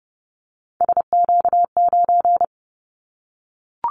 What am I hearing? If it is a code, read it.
Morse code, heard as a keyed tone, about 30 wpm, 700 Hz. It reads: SQ9
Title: Callsign Prefixes for Top 25 DXCC - 30wpm